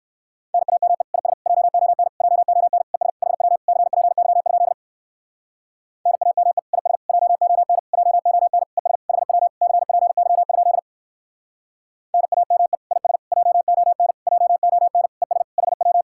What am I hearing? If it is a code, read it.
Morse code, heard as a keyed tone, about 56 wpm, 700 Hz. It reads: NAME IS JON JON ES HW CPY? NAME IS JON JON ES HW CPY? NAME IS JON JON ES HW
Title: Speed-Racing Common QSO Phrases Encore - 45wpm